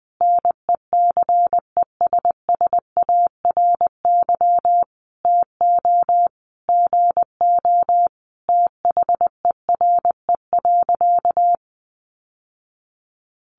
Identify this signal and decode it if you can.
Morse code, heard as a keyed tone, about 20 wpm, 700 Hz. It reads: NECESSARY TO GO THERE.
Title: Sentences from Top 800 Words 20wpm